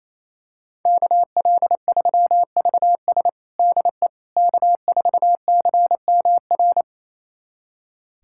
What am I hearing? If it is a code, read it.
Morse code, heard as a keyed tone, about 28 wpm, 700 Hz. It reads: KL3VS DE K4CMR
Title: USA Callsign DE Callsign - 5 Characters - 28wpm